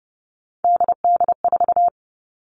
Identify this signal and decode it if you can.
Morse code, heard as a keyed tone, about 30 wpm, 700 Hz. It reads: DD4
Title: Three Challenging Letter-Number Combinations 30wpm